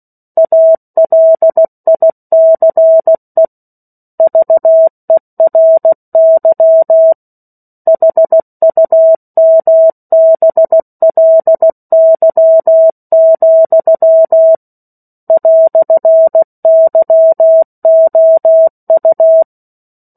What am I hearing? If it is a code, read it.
Morse code, heard as a keyed tone, about 16 wpm, 650 Hz. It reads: ALICE VERY HUMBLY, "YOU
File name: 03-Chapter-03 16 WPM